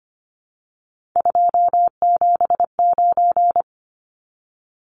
Morse code, heard as a keyed tone, about 25 wpm, 700 Hz. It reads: 279
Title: Numbers - 3-Digit 25wpm